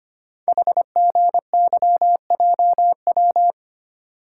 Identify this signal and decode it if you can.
Morse code, heard as a keyed tone, about 25 wpm, 690 Hz. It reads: HGYJW
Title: Code Groups - 5 Letters - 25wpm